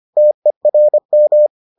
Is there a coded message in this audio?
Yes: Morse code from 0.2 to 1.5 s, heard as a keyed tone, about 25 wpm, 600 Hz. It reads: TERM